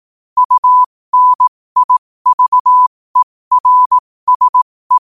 Morse code, heard as a keyed tone, about 18 wpm, spaced as for 15 wpm, 1000 Hz. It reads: UNIVERSE
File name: qotdfiles / 15wpm